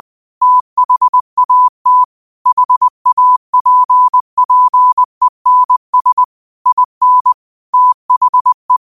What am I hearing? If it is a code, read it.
Morse code, heard as a keyed tone, about 20 wpm, 1000 Hz. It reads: THAT HAPPENS IN THE
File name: qotdfiles / 20wpm